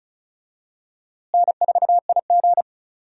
Morse code, heard as a keyed tone, about 36 wpm, 700 Hz. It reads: N4IG
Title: USA Call Signs - 4 Characters - Repeated 2x - 35wpm